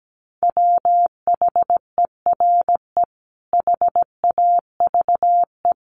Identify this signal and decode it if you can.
Morse code, heard as a keyed tone, about 17 wpm, 700 Hz. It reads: WHERE HAVE
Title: Sentences from Top 200 Words 17wpm